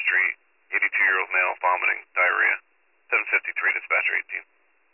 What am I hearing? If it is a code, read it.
Emergency Alert System (SAME) tones, heard as harsh buzzing bursts, with speech. It is heard from 0.7 to 3.8 s.